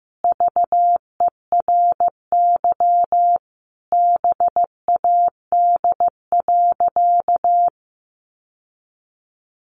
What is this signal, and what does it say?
Morse code, heard as a keyed tone, about 15 wpm, 700 Hz. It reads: VERY BAD.
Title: Sentences from Top 700 Words 15wpm